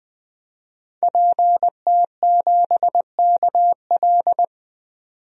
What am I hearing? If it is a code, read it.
Morse code, heard as a keyed tone, about 20 wpm, 700 Hz. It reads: PT7KL
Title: DXCC-100 Call Signs - 5 Characters - 20wpm